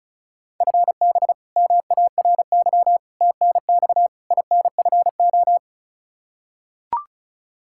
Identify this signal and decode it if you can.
Morse code, heard as a keyed tone, about 36 wpm, 700 Hz. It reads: FB MARY TNX INFO
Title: Practice QSO - Exchange 3 - More Info - 35wpm